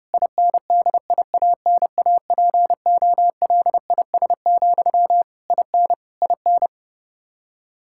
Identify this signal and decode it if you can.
Morse code, heard as a keyed tone, about 28 wpm, 700 Hz. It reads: INDIANAPOLIS, IN IN
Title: K1USN SST QSO Practice 30wpm